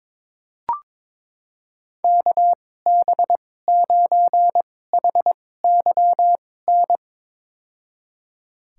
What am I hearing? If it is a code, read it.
Morse code, heard as a keyed tone, about 22 wpm, spaced as for 16 wpm, 700 Hz. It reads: KB9HYN
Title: USA Call Signs - 2x Character Spacing - 22wpm